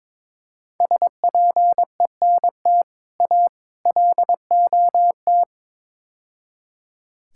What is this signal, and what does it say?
Morse code, heard as a keyed tone, about 22 wpm, 700 Hz. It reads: SPENT A LOT
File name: practice / 22wpm / Build A Phrase - 5-Word Phrases - 22wpm